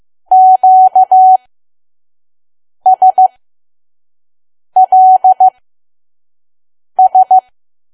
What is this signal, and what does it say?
Morse code, heard as a keyed tone, about 15 wpm, spaced as for 5 wpm, 750 Hz. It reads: QSLS
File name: Archive / 5 WPM files / PM